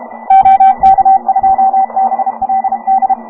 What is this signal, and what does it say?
Morse code, heard as a keyed tone, about 30 wpm, 750 Hz. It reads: OK2BLD